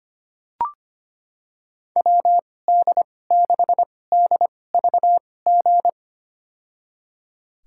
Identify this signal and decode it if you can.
Morse code, heard as a keyed tone, about 25 wpm, spaced as for 18 wpm, 700 Hz. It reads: WD6DVG
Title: USA Call Signs - 6 Characters - 2x Character Spacing - 25wpm